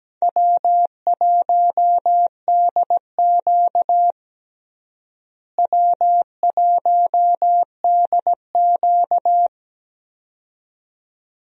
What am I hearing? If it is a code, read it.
Morse code, heard as a keyed tone, about 17 wpm, 700 Hz. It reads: W1DQ W1DQ
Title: USA Call Signs - 4 Characters - Repeated 2x - 17wpm